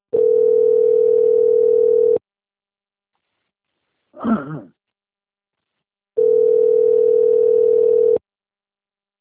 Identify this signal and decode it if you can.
Ringback tone (North America), 440 + 480 Hz.